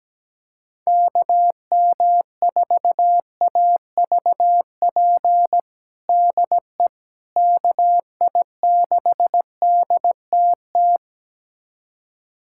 Morse code, heard as a keyed tone, about 17 wpm, 700 Hz. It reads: KM4AVP DE KI6DTT
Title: USA Callsign DE Callsign - 6 Characters - 17wpm